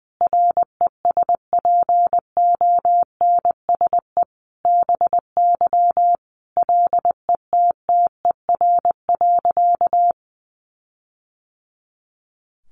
Morse code, heard as a keyed tone, about 20 wpm, 700 Hz. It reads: RESPONSE BY LETTER.
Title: Sentences from Top 1400 Words 20wpm